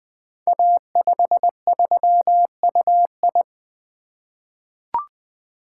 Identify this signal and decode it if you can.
Morse code, heard as a keyed tone, about 20 wpm, 700 Hz. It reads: A53UI
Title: Code Groups - 5 Letter:Numbers - 20wpm